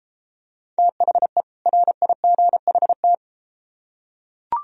Morse code, heard as a keyed tone, about 33 wpm, 700 Hz. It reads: THE RIGHT